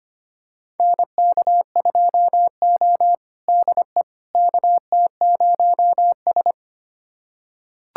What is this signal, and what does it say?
Morse code, heard as a keyed tone, about 25 wpm, 700 Hz. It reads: NK2O DE KT0S
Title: USA Callsign DE Callsign - 4 Characters - 25wpm